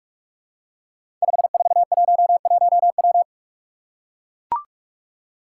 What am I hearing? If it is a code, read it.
Morse code, heard as a keyed tone, about 44 wpm, 700 Hz. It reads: 5411W
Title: Five Challenging Letter-Number Combinations 45wpm